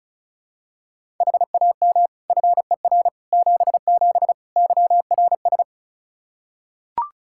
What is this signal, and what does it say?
Morse code, heard as a keyed tone, about 36 wpm, 700 Hz. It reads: HAM FER 77 YRS
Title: K1USN SST QSO Practice 35wpm